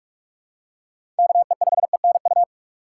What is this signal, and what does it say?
Morse code, heard as a keyed tone, about 43 wpm, 700 Hz. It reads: KE5ENU
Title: USA Call Signs - 6 Characters - 3x Character Spacing - 45wpm